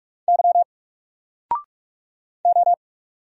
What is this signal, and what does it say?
Morse code, heard as a keyed tone, about 44 wpm, 700 Hz. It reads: Y O
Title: Single Letter-Number Alt 45wpm